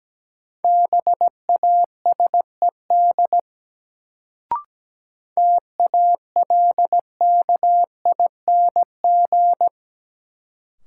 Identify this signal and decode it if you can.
Morse code, heard as a keyed tone, about 17 wpm, 700 Hz. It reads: BASED TALKING
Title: Sets of 2 Words - New Word Drill - 17wpm